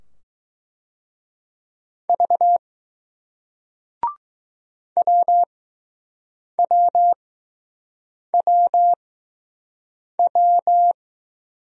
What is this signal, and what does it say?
Morse code, heard as a keyed tone, about 20 wpm, spaced as for 6 wpm, 700 Hz. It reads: V WWWW